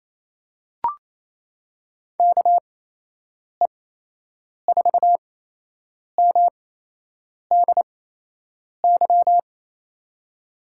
Morse code, heard as a keyed tone, about 28 wpm, spaced as for 8 wpm, 700 Hz. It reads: KE4MDY
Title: USA Call Signs - 6 Characters - 8x Character Spacing - 28wpm